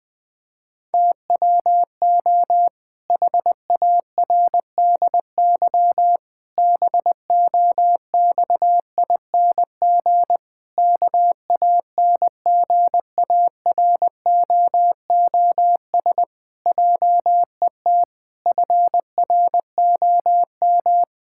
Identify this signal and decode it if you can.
Morse code, heard as a keyed tone, about 20 wpm, 700 Hz. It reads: TWO HARDY BOXING KANGAROOS JET FROM
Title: Pangram Challenge v2 - 20wpm